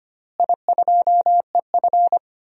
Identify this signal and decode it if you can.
Morse code, heard as a keyed tone, about 25 wpm, 700 Hz. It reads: I2EF
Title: DXCC-100 Call Signs - 4 Characters - Repeated 3x - 25wpm